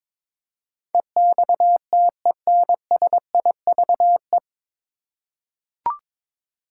Morse code, heard as a keyed tone, about 22 wpm, 700 Hz. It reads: EXTENSIVE
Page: Morse Code Ninja practice 22wpm